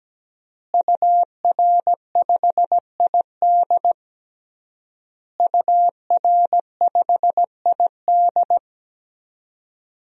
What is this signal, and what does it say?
Morse code, heard as a keyed tone, about 17 wpm, 700 Hz. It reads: UR5ID UR5ID
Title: DXCC-100 Call Signs - 5 Characters - Repeated 3x - 17wpm